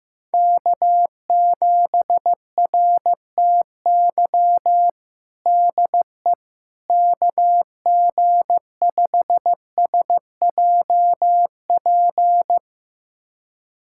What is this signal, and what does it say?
Morse code, heard as a keyed tone, about 15 wpm, 700 Hz. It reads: K7RTY DE KG5SJP
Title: USA Callsign DE Callsign 15wpm